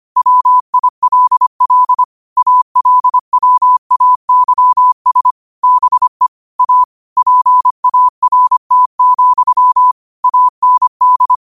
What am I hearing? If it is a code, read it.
Morse code, heard as a keyed tone, about 25 wpm, 1000 Hz. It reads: WILL ALWAYS BE A PART, AND